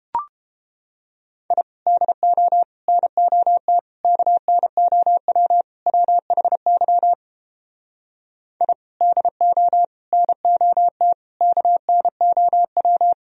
Morse code, heard as a keyed tone, about 31 wpm, 700 Hz. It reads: I DO NOT KNOW WHY I DO NOT KNOW